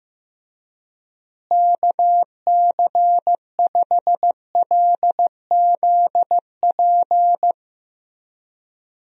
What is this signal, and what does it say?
Morse code, heard as a keyed tone, about 15 wpm, 700 Hz. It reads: KC5LZP
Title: USA Call Signs - 6 Characters - Repeated 2x - 15wpm